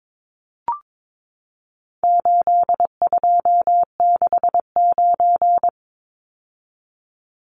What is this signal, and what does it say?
Morse code, heard as a keyed tone, about 22 wpm, 700 Hz. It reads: 8269
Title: Numbers - 4-Digit 22wpm